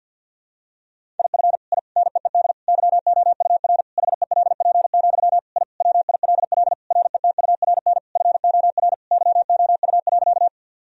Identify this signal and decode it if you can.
Morse code, heard as a keyed tone, about 50 wpm, 700 Hz. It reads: IF I NEED YOUR HELP, I WILL RETURN FOR YOU.
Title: Sentences from Top 1000 Words - Review Part 1 - 50wpm